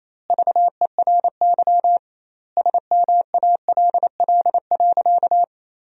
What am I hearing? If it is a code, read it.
Morse code, heard as a keyed tone, about 28 wpm, 700 Hz. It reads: VERY SMALL.
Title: 2x Word-Spacing - Sentences from Top 400 Words - 28wpm